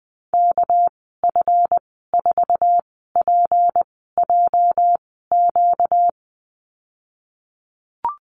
Morse code, heard as a keyed tone, about 20 wpm, spaced as for 15 wpm, 700 Hz. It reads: KF4PJQ
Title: 245 Lesson 39g - Letter z - Call Signs 20wpm